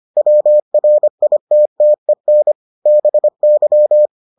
Morse code, heard as a keyed tone, about 25 wpm, 600 Hz. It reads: WRITTEN BY